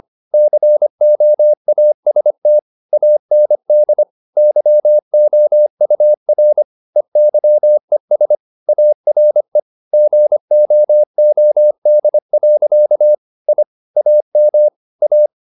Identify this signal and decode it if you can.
Morse code, heard as a keyed tone, about 25 wpm, 600 Hz. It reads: COAST AND YOUR EYES ARE GOOD. I AM A
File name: the old man and the sea part 1 25wpm